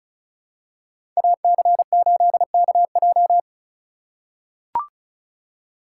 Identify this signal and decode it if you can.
Morse code, heard as a keyed tone, about 36 wpm, 700 Hz. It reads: AC8KJ
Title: USA Call Signs - 5 Characters - 3x Character Spacing - 35wpm